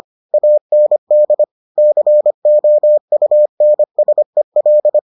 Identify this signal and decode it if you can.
Morse code, heard as a keyed tone, about 25 wpm, 600 Hz. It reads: AND COUNSEL